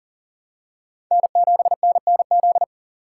Morse code, heard as a keyed tone, about 40 wpm, 700 Hz. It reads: N7NNZ
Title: USA Call Signs - 5 Characters - 2x Character Spacing - 40wpm